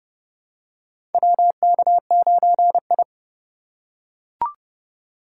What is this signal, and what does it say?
Morse code, heard as a keyed tone, about 30 wpm, 700 Hz. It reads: WK9I
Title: USA Call Signs - 4 Characters - 2x Character Spacing - 30wpm